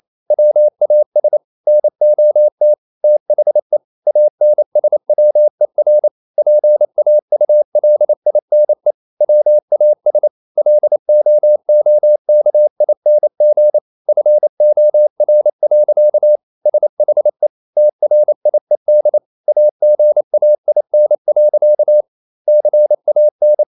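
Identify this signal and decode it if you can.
Morse code, heard as a keyed tone, about 28 wpm, 600 Hz. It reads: WAS NOT THE ANSWER PAULINE WAS LOOKING FOR. SHE TRIED AGAIN. CAN
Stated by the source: Morse code practice at 28 wpm, text of Ken Follett's novel Never in English, from CW med Gnister.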